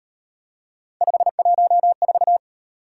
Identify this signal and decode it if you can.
Morse code, heard as a keyed tone, about 37 wpm, 700 Hz. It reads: 514